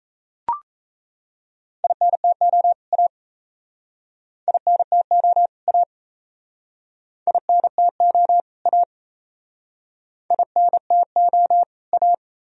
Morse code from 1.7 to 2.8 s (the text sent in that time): INTO